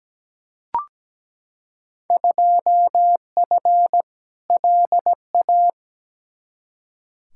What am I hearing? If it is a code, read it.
Morse code, heard as a keyed tone, about 17 wpm, 700 Hz. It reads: 2F LA
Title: Field Day Exchanges - 17wpm